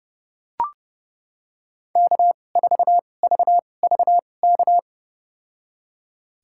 Morse code, heard as a keyed tone, about 30 wpm, spaced as for 22 wpm, 700 Hz. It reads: K4VVK
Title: USA Call Signs - 5 Characters - 2x Character Spacing - 30wpm